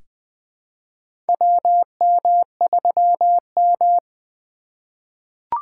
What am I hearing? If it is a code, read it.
Morse code, heard as a keyed tone, about 20 wpm, 700 Hz. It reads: WM3M
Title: USA Call Signs - 4 Characters - 3x Character Spacing - 20wpm